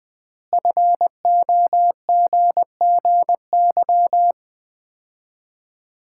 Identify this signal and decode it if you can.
Morse code, heard as a keyed tone, about 21 wpm, 700 Hz. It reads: FOGGY
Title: Common QSO Phrases - 3x Word-Spacing - 20wpm